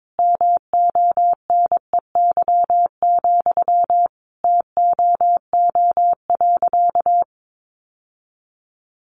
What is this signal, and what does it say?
Morse code, heard as a keyed tone, about 22 wpm, 700 Hz. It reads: MONEY, TOO.
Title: Sentences from Top 900 Words 22wpm